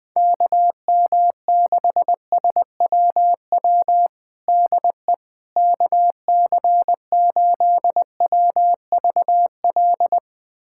Morse code, heard as a keyed tone, about 20 wpm, 700 Hz. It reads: KM6SWW DE KC8WVL
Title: USA Callsign DE Callsign 20wpm